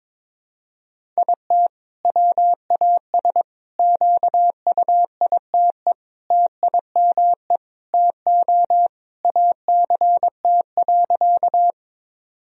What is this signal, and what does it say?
Morse code, heard as a keyed tone, about 22 wpm, 700 Hz. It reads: IT WAS QUITE TIME TO ACT.
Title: Sentences from Top 1000 Words - Review Part 1 - 22wpm